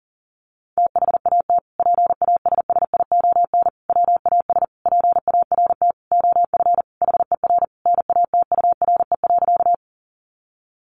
Morse code, heard as a keyed tone, about 40 wpm, 700 Hz. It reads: THAT PASSION WAS PART OF HER NATURE.
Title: Sentences from Top 1300 Words 40wpm